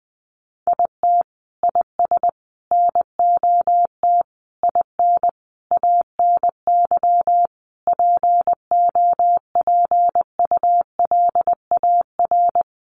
Morse code, heard as a keyed tone, about 19 wpm, 700 Hz. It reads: IT IS NOT IN ANY POPULAR
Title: Sentences from Top 1200 Words 20wpm